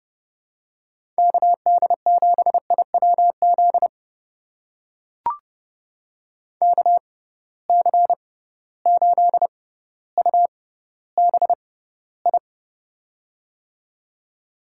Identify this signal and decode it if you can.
Morse code, heard as a keyed tone, about 31 wpm, spaced as for 15 wpm, 700 Hz. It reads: KD7IWZ KC8UBI
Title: USA Call Signs - 6 Characters - 6x Character Spacing - 30wpm